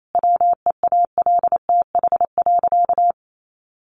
Morse code, heard as a keyed tone, about 28 wpm, 700 Hz. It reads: WEALTH.
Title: Sentences from Top 1200 Words 28wpm